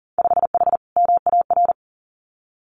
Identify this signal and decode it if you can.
Morse code, heard as a keyed tone, about 40 wpm, 700 Hz. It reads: 5H MAR
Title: Winter Field Day Exchanges Encore 40wpm